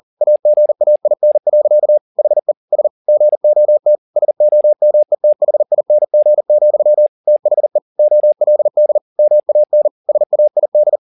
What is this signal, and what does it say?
Morse code, heard as a keyed tone, about 40 wpm, 600 Hz. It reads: AGAIN. HE S GOT SOMETHING, THE OLD MAN SAID